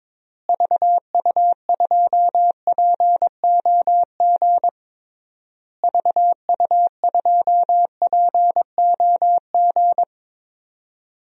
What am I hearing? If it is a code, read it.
Morse code, heard as a keyed tone, about 22 wpm, 700 Hz. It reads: VU2POG VU2POG
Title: DXCC-100 Call Signs - 6 Characters - Repeated 3x - 22wpm